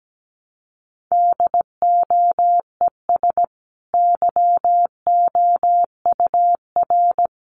Morse code, heard as a keyed tone, about 17 wpm, 700 Hz. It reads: DOES YOUR
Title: Sentences from Top 1200 Words 17wpm